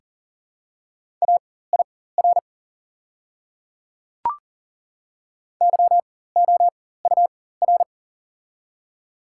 Morse code, heard as a keyed tone, about 40 wpm, spaced as for 19 wpm, 700 Hz. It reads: AIR YOUR
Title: 4x Character Spacing - Top 200 Words - 40wpm